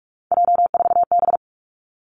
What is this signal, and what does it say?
Morse code, heard as a keyed tone, about 46 wpm, 700 Hz. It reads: J4B
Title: Three Challenging Letter-Number Combinations 45wpm